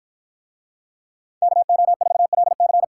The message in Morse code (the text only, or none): KO4LC